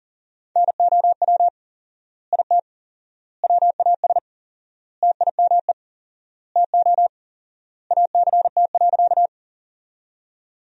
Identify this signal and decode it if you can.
Morse code, heard as a keyed tone, about 40 wpm, 700 Hz. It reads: NOW IT WAS TIME TO ACT.
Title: 4x Word-Spacing - Sentences from Top 200 Words - 40wpm